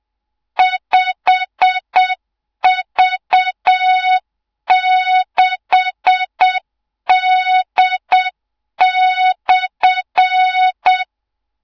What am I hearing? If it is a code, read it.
Morse code, heard as a keyed tone, about 7 wpm, 3700 Hz, with noise. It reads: 5V6D/
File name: Morze-HA2MN / hang / 24 GYAKORLAT